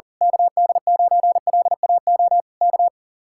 Morse code, heard as a keyed tone, about 37 wpm, 700 Hz. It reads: KD9LAO K
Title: Speed-Racing Common QSO Phrases - 40wpm